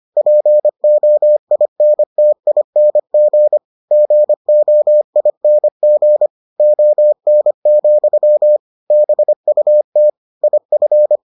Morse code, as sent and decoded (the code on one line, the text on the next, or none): .--. --- .. -. - .. -. --. / --. --- .. -. --. / --- -. --..-- / -... ..- - / .. ..-.
POINTING GOING ON, BUT IF